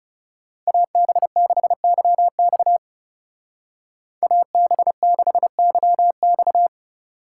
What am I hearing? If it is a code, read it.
Morse code, heard as a keyed tone, about 39 wpm, 700 Hz. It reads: AB6YX AB6YX